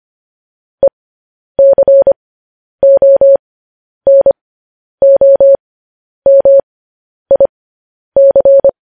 Morse code, heard as a keyed tone, about 25 wpm, spaced as for 10 wpm, 550 Hz. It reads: ECONOMIC
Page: news headlines in morse code at 10 wpm